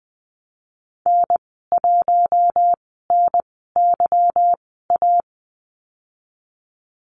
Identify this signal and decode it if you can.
Morse code, heard as a keyed tone, about 20 wpm, spaced as for 15 wpm, 700 Hz. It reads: N1NYA
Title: 161 Lesson 26g - Letter g - Call Signs 20wpm